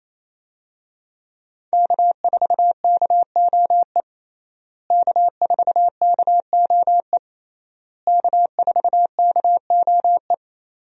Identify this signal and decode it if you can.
Morse code, heard as a keyed tone, about 28 wpm, 700 Hz. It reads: K4KOE K4KOE K4KOE
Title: USA Call Signs - 5 Characters - Repeated 3x - 28wpm